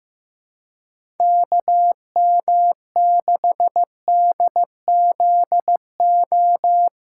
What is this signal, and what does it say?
Morse code, heard as a keyed tone, about 15 wpm, 700 Hz. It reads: KM6DZO
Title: USA Call Signs - 2x Character Spacing - 15wpm